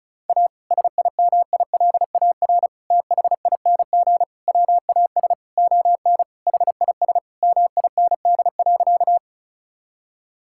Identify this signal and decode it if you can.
Morse code, heard as a keyed tone, about 36 wpm, 700 Hz. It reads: A SIMILAR THING WAS ON HIS MIND.
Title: Sentences from Top 1000 Words - Review Part 5 - 35wpm